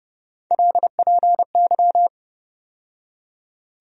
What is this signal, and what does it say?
Morse code, heard as a keyed tone, about 30 wpm, 700 Hz. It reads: LPY
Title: USA Call Sign Postfixes - 3 Letters 30wpm